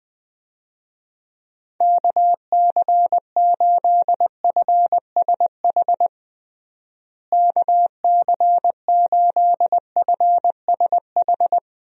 Morse code, heard as a keyed tone, about 20 wpm, 700 Hz. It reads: KC8FSH KC8FSH
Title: USA Call Signs - 6 Characters - Repeated 2x - 20wpm